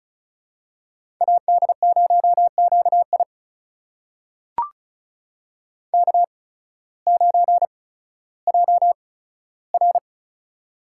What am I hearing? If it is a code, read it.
Morse code, heard as a keyed tone, about 36 wpm, spaced as for 16 wpm, 700 Hz. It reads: AD0QI K9JR
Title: USA Call Signs - 8x Character Spacing - 35wpm